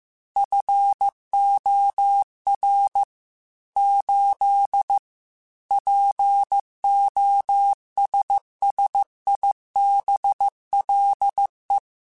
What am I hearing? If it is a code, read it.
Morse code, heard as a keyed tone, about 15 wpm, 800 Hz. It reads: FOR 8 POSSIBLE
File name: Technical Subjects / Technical Subjects-mp3s / AND Gate @13